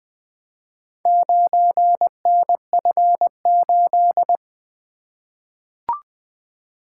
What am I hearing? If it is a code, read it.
Morse code, heard as a keyed tone, about 20 wpm, 700 Hz. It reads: 9NF8
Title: Four Letter-Number Combinations - 20wpm